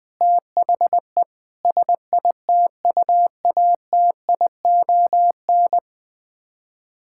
Morse code, heard as a keyed tone, about 20 wpm, 700 Hz. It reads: THE SITUATION
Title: Sets of 9 Words - 20wpm